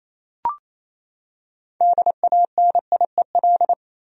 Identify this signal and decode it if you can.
Morse code, heard as a keyed tone, about 28 wpm, 700 Hz. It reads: DANIEL
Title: USA Common Names - 28wpm